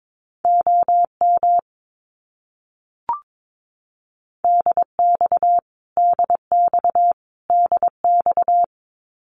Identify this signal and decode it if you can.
Morse code, heard as a keyed tone, about 22 wpm, 700 Hz. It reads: OM DX DX DX
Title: Abbreviations Repeated 3x - 22wpm